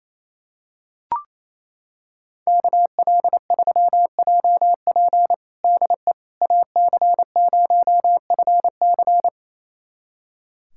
Morse code, heard as a keyed tone, about 28 wpm, 700 Hz. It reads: KL3JP DE AC0FC
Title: USA Callsign DE Callsign - 5 Characters - 28wpm